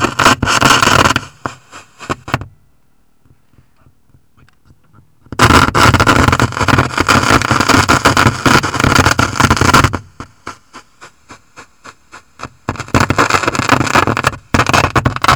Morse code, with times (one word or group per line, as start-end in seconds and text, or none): none